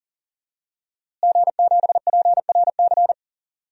Morse code, heard as a keyed tone, about 40 wpm, 700 Hz. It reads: G7PRC
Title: DXCC-100 Call Signs - 5 Characters - Repeated 2x - 40wpm